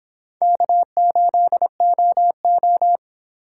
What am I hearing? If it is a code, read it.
Morse code, heard as a keyed tone, about 24 wpm, 700 Hz. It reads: K8OO
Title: Speed Racing - USA Call Signs 22wpm